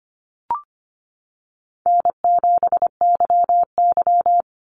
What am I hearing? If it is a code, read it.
Morse code, heard as a keyed tone, about 25 wpm, 700 Hz. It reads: N7YY
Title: Call Signs Repeated 2x - 25wpm